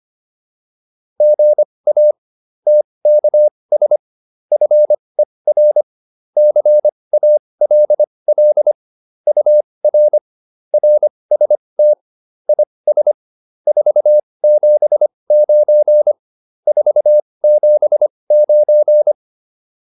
Morse code, heard as a keyed tone, about 25 wpm, spaced as for 20 wpm, 600 Hz. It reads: GA TKS FER CALL UR RST IS 479 479
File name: QSO401a 20